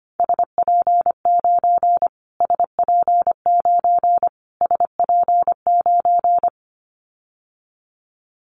Morse code, heard as a keyed tone, about 25 wpm, 700 Hz. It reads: SP9 SP9 SP9
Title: Callsign Prefixes for Top 25 DXCC - Repeated 3x - 25wpm